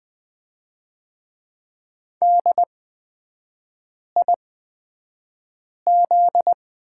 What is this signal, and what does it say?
Morse code, heard as a keyed tone, about 20 wpm, spaced as for 5 wpm, 700 Hz. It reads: DIZ
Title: 255 Lesson 41e - Period - Three Characters 20wpm